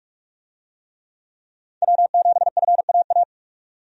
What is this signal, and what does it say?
Morse code, heard as a keyed tone, about 46 wpm, 700 Hz. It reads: W7FAA
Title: USA Call Signs - 5 Characters - 45wpm